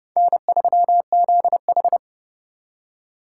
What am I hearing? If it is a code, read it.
Morse code, heard as a keyed tone, about 37 wpm, 700 Hz. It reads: N3ZH
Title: Speed Racing - USA Call Signs 30wpm